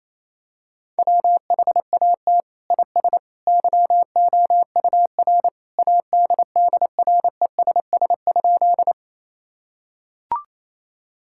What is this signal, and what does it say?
Morse code, heard as a keyed tone, about 28 wpm, 700 Hz. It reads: WHAT IS YOUR ADDRESS?
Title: Sentences from Top 1000 Words - Review Part 6 - 28wpm